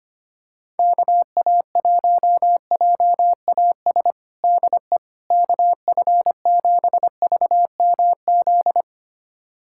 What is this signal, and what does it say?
Morse code, heard as a keyed tone, about 25 wpm, 700 Hz. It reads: KA1JAS DE KF7VMZ
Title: USA Callsign DE Callsign - 6 Characters - 25wpm